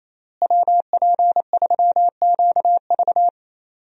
Morse code, heard as a keyed tone, about 28 wpm, 700 Hz. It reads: WP3QV
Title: USA Call Signs - 5 Characters - Repeated 3x - 28wpm